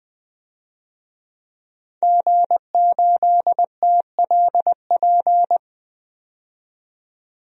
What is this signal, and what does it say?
Morse code, heard as a keyed tone, about 20 wpm, 700 Hz. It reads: G8TLP
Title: DXCC-100 Call Signs - 5 Characters - 20wpm